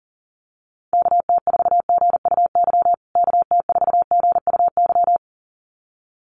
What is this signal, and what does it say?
Morse code, heard as a keyed tone, about 40 wpm, 700 Hz. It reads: KT4GUY KT4GUY
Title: USA Call Signs Repeated 2x 40wpm